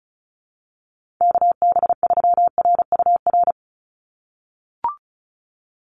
Morse code, heard as a keyed tone, about 36 wpm, 700 Hz. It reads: KB3RUR
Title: USA Call Signs 35wpm